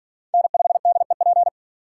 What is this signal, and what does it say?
Morse code, heard as a keyed tone, about 53 wpm, 700 Hz. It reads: N5DEP